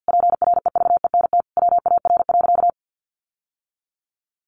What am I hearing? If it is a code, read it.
Morse code, heard as a keyed tone, about 50 wpm, 700 Hz. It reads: PREVENT WAR.
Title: Sentences from Top 1100 Words 50wpm